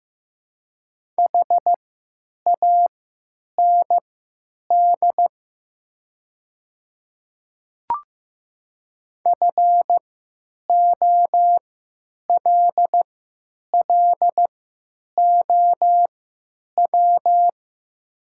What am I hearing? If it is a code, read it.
Morse code, heard as a keyed tone, about 15 wpm, spaced as for 9 wpm, 700 Hz. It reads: HAND FOLLOW
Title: 3x Character Spacing - Top 200 Words - 15wpm